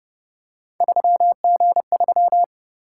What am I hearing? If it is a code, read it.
Morse code, heard as a keyed tone, about 29 wpm, 700 Hz. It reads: 3G3
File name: Speed-Racing - Three Letter-Number Combinations - 25wpm